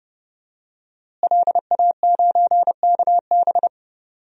Morse code, heard as a keyed tone, about 31 wpm, 700 Hz. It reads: LA9KB
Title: DXCC-100 Call Signs - 5 Characters - Repeated 3x - 30wpm